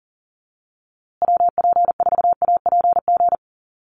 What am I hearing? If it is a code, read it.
Morse code, heard as a keyed tone, about 40 wpm, 700 Hz. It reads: WP4APG
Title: USA Call Signs Repeated 3x 40wpm